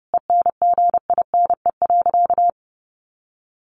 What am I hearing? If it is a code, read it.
Morse code, heard as a keyed tone, about 30 wpm, 700 Hz. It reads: ENGINE.
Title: Sentences from Top 1500 Words 30wpm